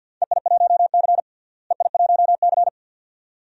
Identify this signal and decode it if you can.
Morse code, heard as a keyed tone, about 53 wpm, 700 Hz. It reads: EI1C EI1C
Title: DXCC-100 Call Signs - 4 Characters - Repeated 2x - 50wpm